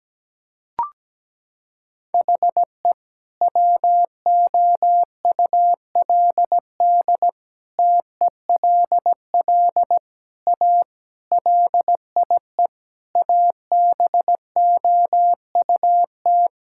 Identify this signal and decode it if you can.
Morse code, heard as a keyed tone, about 17 wpm, 700 Hz. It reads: HE WOULD TELL A LIE ABOUT
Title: Sentences from Top 1600 Words 17wpm